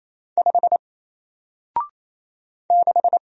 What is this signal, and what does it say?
Morse code, heard as a keyed tone, about 28 wpm, 700 Hz. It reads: 5 6
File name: Single-Letter-Number-28wpm